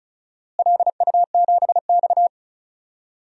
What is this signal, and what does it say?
Morse code, heard as a keyed tone, about 34 wpm, 700 Hz. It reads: LU7X
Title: DXCC-100 Call Signs - 4 Characters - Repeated 3x - 35wpm